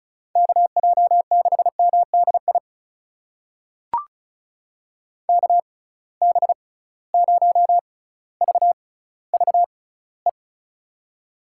Morse code, heard as a keyed tone, about 36 wpm, spaced as for 18 wpm, 700 Hz. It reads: KJ6MDI KB0VVE
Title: USA Call Signs - 6 Characters - 6x Character Spacing - 35wpm